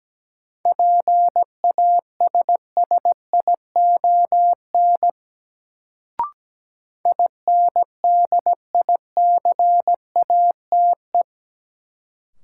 Morse code, heard as a keyed tone, about 17 wpm, 700 Hz. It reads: PASSION INDICATE
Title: Sentences from Top 1300 Words - New Word Drill Alt - 17wpm